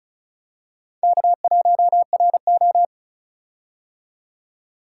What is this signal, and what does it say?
Morse code, heard as a keyed tone, about 36 wpm, 700 Hz. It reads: K1RO
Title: K1USN SST QSO Practice - 35wpm